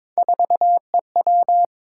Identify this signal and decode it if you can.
Morse code, heard as a keyed tone, about 22 wpm, 700 Hz. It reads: 4EW
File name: practice / 22wpm / Three Letter-Number Combinations - 22wpm